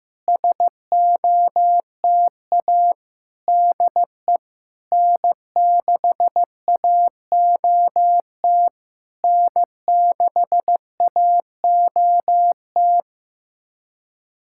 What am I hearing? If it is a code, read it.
Morse code, heard as a keyed tone, about 15 wpm, 690 Hz. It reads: SOTA DE N6AOT N6AOT
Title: SOTA QSO Practice 15wpm